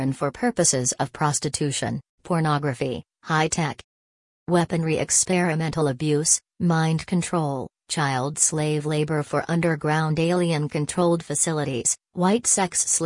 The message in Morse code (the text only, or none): none